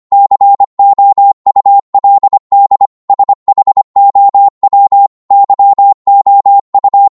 Morse code, heard as a keyed tone, about 25 wpm, 800 Hz. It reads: COULD SHOW YOU